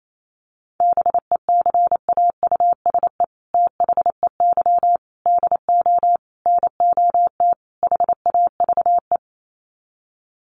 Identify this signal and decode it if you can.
Morse code, heard as a keyed tone, about 28 wpm, 700 Hz. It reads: BECAUSE THEY DO NOT HAVE
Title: Sets of 5 Words Encore2 28wpm